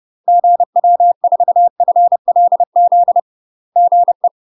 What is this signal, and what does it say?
Morse code, heard as a keyed tone, about 30 wpm, 700 Hz. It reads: GW4FLZ GE
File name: UK MASTER QSOs TRAINING / 30WPM / UK MASTER QSOs 0018